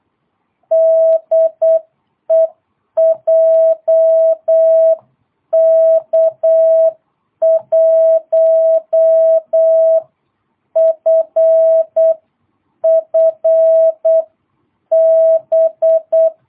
Morse code, heard as a keyed tone, about 8 wpm, spaced as for 7 wpm, 660 Hz. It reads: DE JK1FFB
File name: jh8cbh / 2022cw